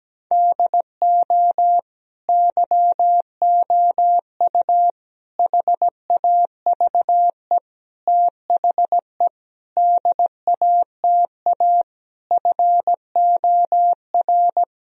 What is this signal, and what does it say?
Morse code, heard as a keyed tone, about 17 wpm, 700 Hz. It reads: DO YOU HAVE THE DATA FOR
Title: Sentences from Top 1000 Words - Review Part 3 - 17wpm